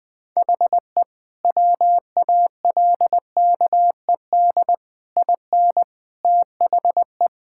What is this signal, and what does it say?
Morse code, heard as a keyed tone, about 20 wpm, 700 Hz. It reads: HE WALKED IN THE
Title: Sentences with 5-Word Phrases - 20wpm